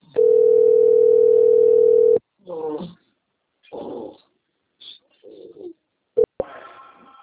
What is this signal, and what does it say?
Ringback tone (North America), 440 + 480 Hz.